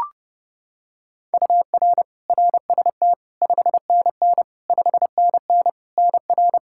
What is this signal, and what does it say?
Morse code, heard as a keyed tone, about 30 wpm, 700 Hz. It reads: UR RST 5NN 5NN NR
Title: Practice QSO - Exchange 1 - Beginning - 30wpm